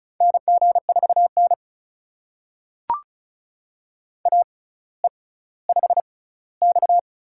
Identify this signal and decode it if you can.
Morse code, heard as a keyed tone, about 36 wpm, spaced as for 18 wpm, 700 Hz. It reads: NG4N AE5X
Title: USA Call Signs - 4 Characters - 6x Character Spacing - 35wpm